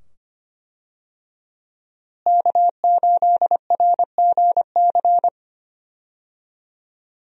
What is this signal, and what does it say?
Morse code, heard as a keyed tone, about 25 wpm, 700 Hz. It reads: K8RGC
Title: USA Call Signs - 5 Characters - Repeated 2x - 25wpm